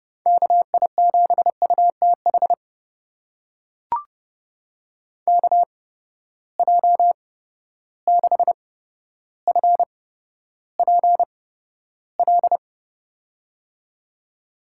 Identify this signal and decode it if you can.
Morse code, heard as a keyed tone, about 32 wpm, spaced as for 13 wpm, 700 Hz. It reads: KI7UTH KJ6FPL
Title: USA Call Signs - 6 Characters - 8x Character Spacing - 30wpm